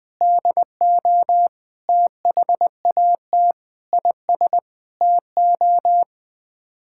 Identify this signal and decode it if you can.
Morse code, heard as a keyed tone, about 20 wpm, 700 Hz. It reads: DO THAT IS TO
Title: Sets of 8 Words - 20wpm